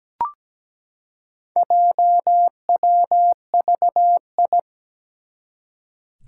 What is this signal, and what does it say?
Morse code, heard as a keyed tone, about 17 wpm, 700 Hz. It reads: JWVI
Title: Four Challenging Letter-Number Combinations 17wpm